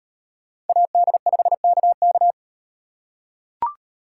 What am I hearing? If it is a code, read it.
Morse code, heard as a keyed tone, about 36 wpm, 700 Hz. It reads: AD5KK